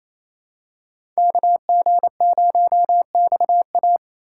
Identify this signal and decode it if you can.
Morse code, heard as a keyed tone, about 28 wpm, 700 Hz. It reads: KG0XA
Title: USA Call Signs - 5 Characters - 4x Character Spacing - 28wpm